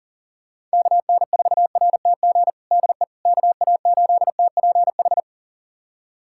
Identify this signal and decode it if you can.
Morse code, heard as a keyed tone, about 40 wpm, 700 Hz. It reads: KN4RTG DE KA8TPH
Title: USA Callsign DE Callsign - 6 Characters - 40wpm